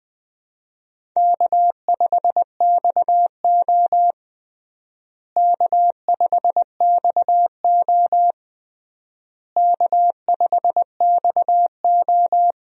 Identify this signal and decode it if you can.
Morse code, heard as a keyed tone, about 20 wpm, 700 Hz. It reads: K5XO K5XO K5XO
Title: USA Call Signs - 4 Characters - Repeated 3x - 20wpm